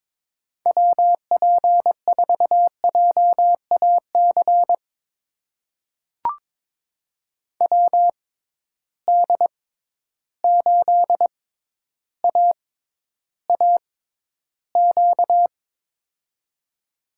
Morse code, heard as a keyed tone, about 22 wpm, spaced as for 11 wpm, 700 Hz. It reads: WP4JAC WD8AAQ
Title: USA Call Signs - 6 Characters - 6x Character Spacing - 22wpm